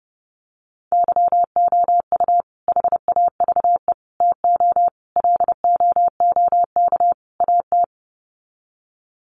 Morse code, heard as a keyed tone, about 30 wpm, 700 Hz. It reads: YOU HAVE TO LOOK AT
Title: Sets of 5 Words Encore 30wpm